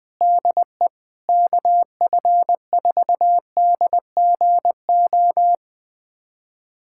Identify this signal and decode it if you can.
Morse code, heard as a keyed tone, about 20 wpm, 700 Hz. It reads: DE KF4DGO
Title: USA Callsign DE Callsign - 6 Characters - 20wpm